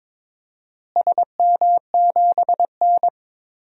Morse code, heard as a keyed tone, about 22 wpm, 700 Hz. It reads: SM7N
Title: DXCC-100 Call Signs - 4 Characters - Repeated 2x - 22wpm